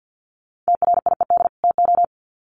Morse code, heard as a keyed tone, about 50 wpm, 700 Hz. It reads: TRIED TO
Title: Sets of 2 Words 50wpm